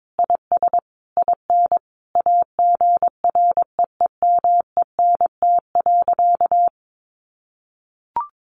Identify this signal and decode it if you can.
Morse code, heard as a keyed tone, about 22 wpm, 700 Hz. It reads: IS IN AGREEMENT.
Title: Sentences from Top 1500 Words 22wpm